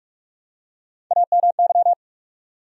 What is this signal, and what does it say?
Morse code, heard as a keyed tone, about 46 wpm, 700 Hz. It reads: AMY
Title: USA Common Names - Max 4 Letters - 45wpm